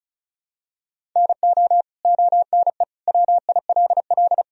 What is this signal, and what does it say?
Morse code, heard as a keyed tone, about 34 wpm, 700 Hz. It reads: NO ONE WILL
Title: Sentences from Top 100 Words and Abbreviations 35wpm